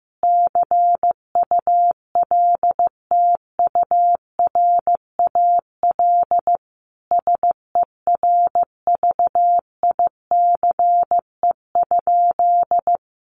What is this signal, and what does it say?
Morse code, heard as a keyed tone, about 15 wpm, 700 Hz. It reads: CULTURAL SERVICE?
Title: Sentences from Top 1500 Words 15wpm